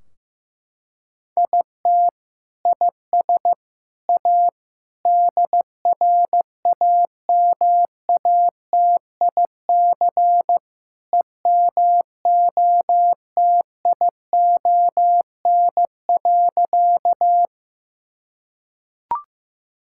Morse code, heard as a keyed tone, about 15 wpm, 700 Hz. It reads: IT IS A DRAMATIC EMOTION.
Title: Sentences from Top 1600 Words 15wpm